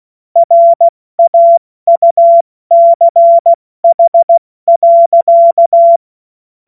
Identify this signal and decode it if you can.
Morse code, heard as a keyed tone, about 16 wpm, 680 Hz. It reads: RAUCH.
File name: Morsekurs / Witze / Witz 14. Juni 2022 70 F80